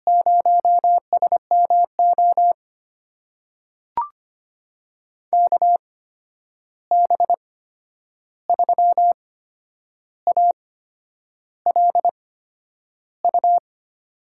Morse code, heard as a keyed tone, about 25 wpm, spaced as for 9 wpm, 700 Hz. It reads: 0SMO KB3ALU
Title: USA Call Signs - 6 Characters - 8x Character Spacing - 25wpm